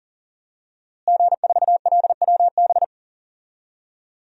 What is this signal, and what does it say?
Morse code, heard as a keyed tone, about 40 wpm, 700 Hz. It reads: G4LWB
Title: DXCC-100 Call Signs - 5 Characters - Repeated 3x - 40wpm